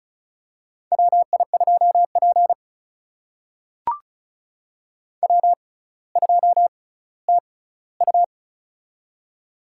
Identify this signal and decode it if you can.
Morse code, heard as a keyed tone, about 36 wpm, spaced as for 18 wpm, 700 Hz. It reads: WI2P W2TU
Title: USA Call Signs - 4 Characters - 6x Character Spacing - 35wpm